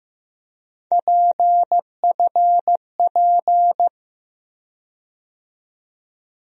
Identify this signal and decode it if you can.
Morse code, heard as a keyed tone, about 15 wpm, 700 Hz. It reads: PFP
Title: USA Call Sign Postfixes - 3 Letters 15wpm